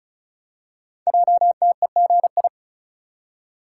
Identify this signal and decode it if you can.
Morse code, heard as a keyed tone, about 35 wpm, 700 Hz. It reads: JTEGI
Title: Code Groups - 5 Letters - 35wpm